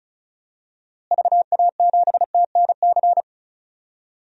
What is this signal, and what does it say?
Morse code, heard as a keyed tone, about 36 wpm, 700 Hz. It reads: VA7TNC
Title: DXCC-100 Call Signs - 6 Characters - Repeated 3x - 35wpm